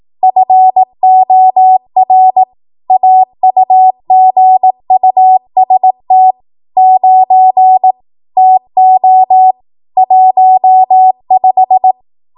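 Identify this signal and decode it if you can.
Morse code, heard as a keyed tone, about 18 wpm, 750 Hz. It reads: FOR AUGUST 9 TO 15
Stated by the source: ARRL 18 wpm Morse code practice archive transmission.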